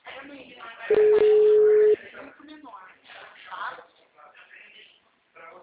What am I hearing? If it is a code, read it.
Ringback tone (Europe, 425 Hz), with speech.